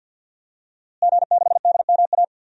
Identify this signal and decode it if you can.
Morse code, heard as a keyed tone, about 50 wpm, 700 Hz. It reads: G6DMA
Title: DXCC-100 Call Signs - 5 Characters - Repeated 3x - 50wpm